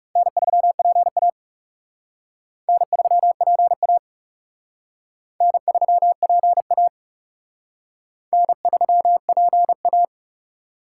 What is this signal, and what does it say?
Morse code, heard as a keyed tone, about 36 wpm, 700 Hz. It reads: N3PA N3PA N3PA N3PA
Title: Speed Racing - USA 1x2 Call Signs 30wpm